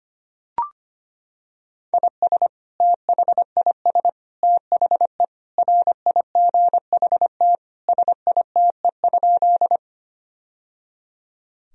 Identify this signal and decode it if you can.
Morse code, heard as a keyed tone, about 25 wpm, 700 Hz. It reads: IS THIS THE RIGHT SITE?
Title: Sentences from Top 1000 Words - Review Part 4 - 25wpm